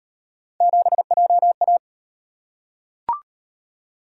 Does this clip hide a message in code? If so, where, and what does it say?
Yes: Morse code from 0.6 to 1.8 s, heard as a keyed tone, about 33 wpm, 700 Hz. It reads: 7JA